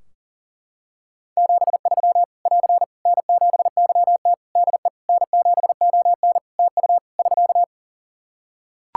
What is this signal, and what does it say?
Morse code, heard as a keyed tone, about 40 wpm, 700 Hz. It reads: 73 <AR> N7YT DE N7ON TU <SK>
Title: Practice QSO - Exchange 4 - Ending - 40wpm